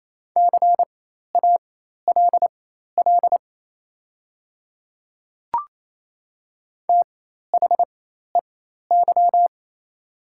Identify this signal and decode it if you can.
Morse code, heard as a keyed tone, about 28 wpm, spaced as for 13 wpm, 700 Hz. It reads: CALL THEY
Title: 4x Character Spacing - Top 300 Words - 28wpm